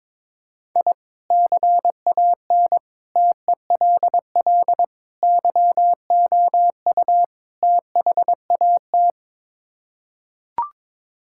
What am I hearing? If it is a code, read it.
Morse code, heard as a keyed tone, about 22 wpm, 700 Hz. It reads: I CAN TELL YOU THAT
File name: practice / 22wpm / Build A Phrase - 5-Word Phrases - 22wpm